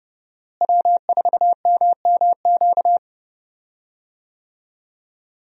Morse code, heard as a keyed tone, about 32 wpm, 700 Hz. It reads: W4MMQ
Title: USA Call Signs - 5 Characters - Repeated 3x - 30wpm